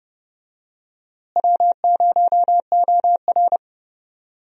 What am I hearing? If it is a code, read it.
Morse code, heard as a keyed tone, about 30 wpm, 700 Hz. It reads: W0OR
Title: USA Call Signs - 4 Characters - Repeated 3x - 30wpm